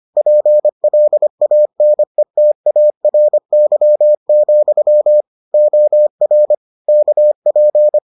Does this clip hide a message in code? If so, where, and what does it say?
Yes: Morse code from 0.2 to 8.0 s, heard as a keyed tone, about 25 wpm, 600 Hz. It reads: PLANETARY, OR KP